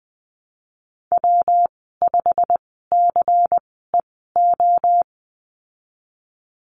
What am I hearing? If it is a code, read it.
Morse code, heard as a keyed tone, about 20 wpm, spaced as for 15 wpm, 700 Hz. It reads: W5CEO
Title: 167 Lesson 27g - Number 7 - Call Signs 20wpm